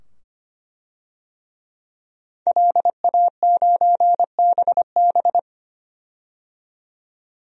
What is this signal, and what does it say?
Morse code, heard as a keyed tone, about 25 wpm, 700 Hz. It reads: LA9BB
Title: DXCC-100 Call Signs - 5 Characters - 25wpm